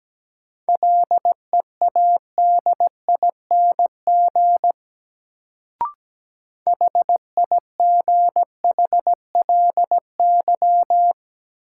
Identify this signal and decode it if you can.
Morse code, heard as a keyed tone, about 17 wpm, 700 Hz. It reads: LEADING HIGHLY
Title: Sentences from Top 1100 Words - New Word Drill Alt - 17wpm